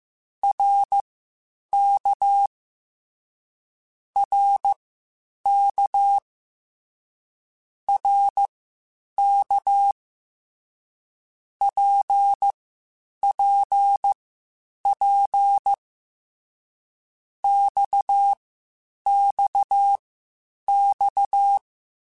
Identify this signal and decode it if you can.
Morse code, heard as a keyed tone, about 15 wpm, spaced as for 8 wpm, 800 Hz. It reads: RK RK RK PPP XXX
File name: Short Runs / Short Runs-mp3s / Balanced @13